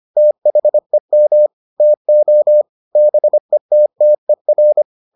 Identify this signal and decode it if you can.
Morse code, heard as a keyed tone, about 25 wpm, 600 Hz. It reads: THEM TO BETTER